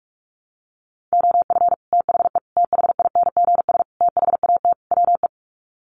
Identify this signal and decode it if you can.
Morse code, heard as a keyed tone, about 44 wpm, 700 Hz. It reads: OF THE THINGS THAT WE
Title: Sets of 5 Words Encore 45wpm